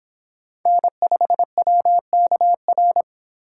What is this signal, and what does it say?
Morse code, heard as a keyed tone, about 26 wpm, 700 Hz. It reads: N5WKR